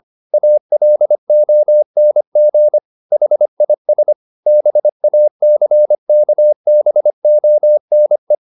Morse code, heard as a keyed tone, about 25 wpm, 600 Hz. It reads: ALONG HIS BACKBONE
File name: the old man and the sea part 1 25wpm